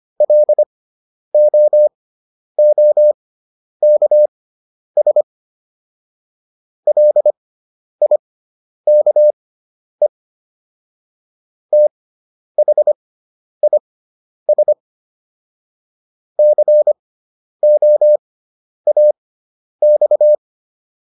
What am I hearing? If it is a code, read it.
Morse code, heard as a keyed tone, about 25 wpm, spaced as for 10 wpm, 600 Hz. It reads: LOOKS LIKE THIS COAX